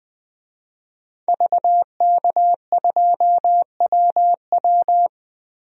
Morse code, heard as a keyed tone, about 21 wpm, 700 Hz. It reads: VK2WW
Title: DXCC-100 Call Signs - 5 Characters - 20wpm